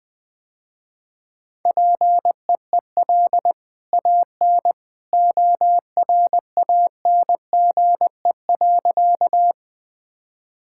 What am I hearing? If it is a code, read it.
Morse code, heard as a keyed tone, about 20 wpm, 700 Hz. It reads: PEEL AN ORANGE.
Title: Easy Sentences with Hints - 20wpm